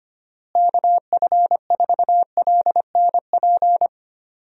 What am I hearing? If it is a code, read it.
Morse code, heard as a keyed tone, about 25 wpm, 700 Hz. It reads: KF4LNP
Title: USA Call Signs - 6 Characters - Repeated 3x - 25wpm